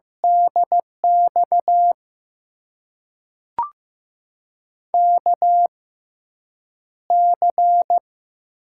Morse code, heard as a keyed tone, about 15 wpm, spaced as for 8 wpm, 700 Hz. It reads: DX KC
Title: USA Call Signs - 6 Characters - 6x Character Spacing - 15wpm